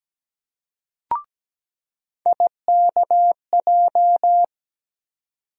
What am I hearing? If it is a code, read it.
Morse code, heard as a keyed tone, about 17 wpm, 700 Hz. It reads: IKJ
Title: USA Call Sign Postfixes - 3 Letters 17wpm